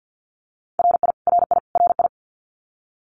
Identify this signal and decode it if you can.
Morse code, heard as a keyed tone, about 50 wpm, 700 Hz. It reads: RI RI RI
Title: US States Repeated 3x 50wpm